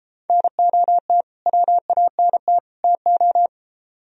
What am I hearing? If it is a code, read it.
Morse code, heard as a keyed tone, about 31 wpm, 700 Hz. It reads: NOT WANT TO